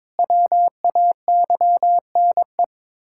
Morse code, heard as a keyed tone, about 22 wpm, 700 Hz. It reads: WAYNE
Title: Practice QSO - Exchange 1 - Beginning - 22wpm